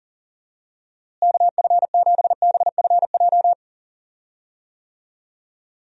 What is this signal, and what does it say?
Morse code, heard as a keyed tone, about 40 wpm, 700 Hz. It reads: KF7BFJ
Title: USA Call Signs - 6 Characters - Repeated 2x - 40wpm